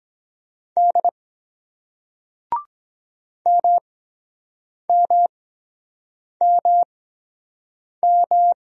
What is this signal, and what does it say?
Morse code, heard as a keyed tone, about 23 wpm, spaced as for 7 wpm, 700 Hz. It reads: D MMMM